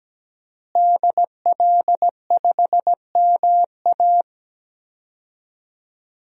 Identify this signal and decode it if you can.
Morse code, heard as a keyed tone, about 17 wpm, 700 Hz. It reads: DL5MA
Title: DXCC-100 Call Signs - 5 Characters - Repeated 3x - 17wpm